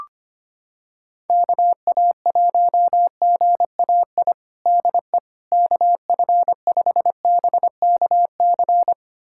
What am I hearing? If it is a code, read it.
Morse code, heard as a keyed tone, about 25 wpm, 700 Hz. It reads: KA1GAI DE KF5BKC
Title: USA Callsign DE Callsign - 6 Characters - 25wpm